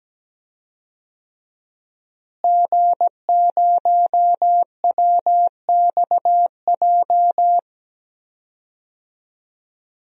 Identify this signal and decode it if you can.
Morse code, heard as a keyed tone, about 17 wpm, 700 Hz. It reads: G0WXJ
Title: DXCC-100 Call Signs - 5 Characters - Repeated 2x - 17wpm